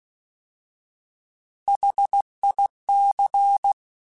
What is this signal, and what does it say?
Morse code, heard as a keyed tone, about 16 wpm, 800 Hz. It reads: HIC
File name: Alphabetical / Alphabetical mp3s / Hs @15